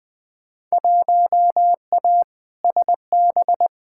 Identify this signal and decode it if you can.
Morse code, heard as a keyed tone, about 20 wpm, 700 Hz. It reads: 1A SB
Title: Field Day Exchanges Encore - 20wpm